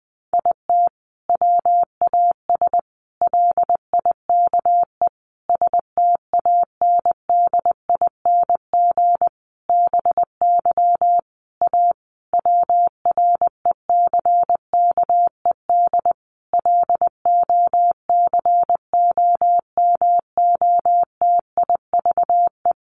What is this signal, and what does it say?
Morse code, heard as a keyed tone, about 20 wpm, 700 Hz. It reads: IT WAS LIKE STANDING BY A WRECKED LOCOMOTIVE
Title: A Sound Of Thunder 20wpm